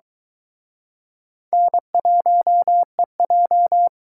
Morse code, heard as a keyed tone, about 22 wpm, 700 Hz. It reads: N1EJ